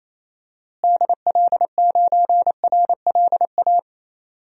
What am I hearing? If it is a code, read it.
Morse code, heard as a keyed tone, about 28 wpm, 700 Hz. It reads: DL9RLA
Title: DXCC-100 Call Signs - 6 Characters - Repeated 3x - 28wpm